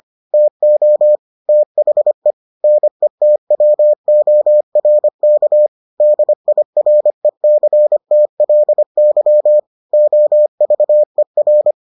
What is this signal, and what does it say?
Morse code, heard as a keyed tone, about 25 wpm, 600 Hz. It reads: TO THE NETWORK DIRECTLY OVER